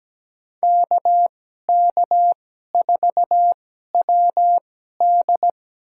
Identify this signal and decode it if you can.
Morse code, heard as a keyed tone, about 17 wpm, spaced as for 12 wpm, 700 Hz. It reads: KK4WD
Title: USA Call Signs - 5 Characters - 2x Character Spacing - 17wpm